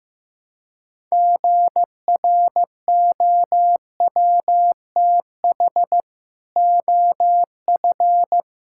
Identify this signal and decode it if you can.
Morse code, heard as a keyed tone, about 15 wpm, 700 Hz. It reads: GROWTH OF
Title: Sentences from Top 1000 Words - Review Part 5 - 15wpm